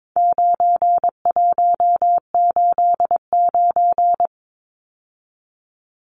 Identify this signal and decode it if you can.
Morse code, heard as a keyed tone, about 22 wpm, 700 Hz. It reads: 9189
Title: Numbers - 4-Digit 22wpm